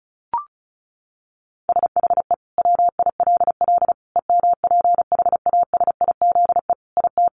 Morse code, heard as a keyed tone, about 34 wpm, 700 Hz. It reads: SHE WILL EMPHASIZE IT
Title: Sentences from Top 1500 Words 35wpm